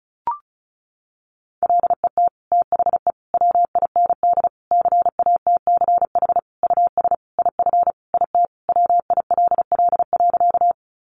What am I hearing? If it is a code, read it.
Morse code, heard as a keyed tone, about 34 wpm, 700 Hz. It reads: LET THE WIND CATCH US IF IT WILL.
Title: Sentences from Top 1500 Words 35wpm